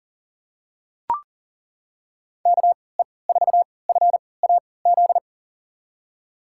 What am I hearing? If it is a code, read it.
Morse code, heard as a keyed tone, about 37 wpm, spaced as for 22 wpm, 700 Hz. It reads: KE4FAZ
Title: USA Call Signs - 6 Characters - 3x Character Spacing - 40wpm